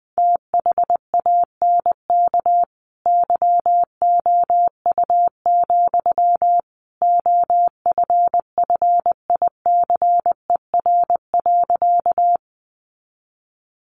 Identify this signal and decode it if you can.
Morse code, heard as a keyed tone, about 20 wpm, 700 Hz. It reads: THANK YOU, OFFICER.
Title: Sentences from Top 1200 Words 20wpm